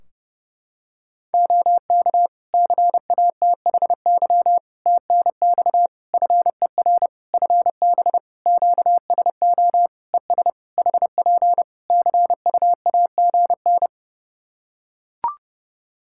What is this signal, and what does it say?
Morse code, heard as a keyed tone, about 30 wpm, 700 Hz. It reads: OK CATHY TNX FER FB QSO ES HP CUAGN
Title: Practice QSO - Exchange 4 - Ending - 30wpm